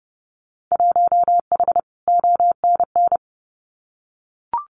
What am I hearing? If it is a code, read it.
Morse code, heard as a keyed tone, about 30 wpm, 700 Hz. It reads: 1H ONN
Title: Winter Field Day Exchanges 30wpm